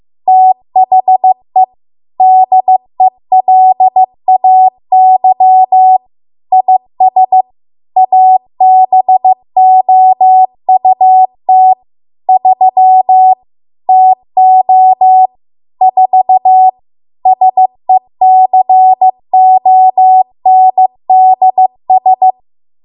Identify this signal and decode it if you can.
Morse code, heard as a keyed tone, about 15 wpm, 750 Hz. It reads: THE DELAY IS ABOUT 3 TO 4 SECONDS